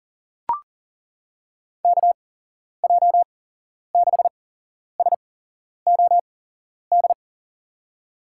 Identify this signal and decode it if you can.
Morse code, heard as a keyed tone, about 40 wpm, spaced as for 11 wpm, 700 Hz. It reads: KJ6SOD
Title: USA Call Signs - 6 Characters - 8x Character Spacing - 40wpm